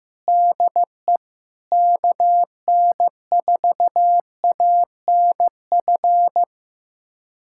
Morse code, heard as a keyed tone, about 15 wpm, 700 Hz. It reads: DE KN4ANF
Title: USA Callsign DE Callsign - 6 Characters - 15wpm